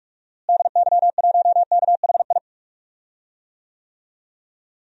Morse code, heard as a keyed tone, about 45 wpm, 700 Hz. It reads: DY1KHI
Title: DXCC-100 Call Signs - 6 Characters - Repeated 2x - 45wpm